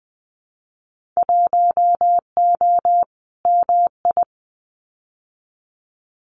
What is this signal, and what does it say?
Morse code, heard as a keyed tone, about 20 wpm, 700 Hz. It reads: 1O MI
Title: Winter Field Day Exchanges Encore 20wpm